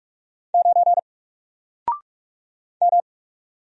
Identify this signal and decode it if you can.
Morse code, heard as a keyed tone, about 44 wpm, 700 Hz. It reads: 9 M
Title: Single Character (Letter:Number:Punctuation:Prosign) 45wpm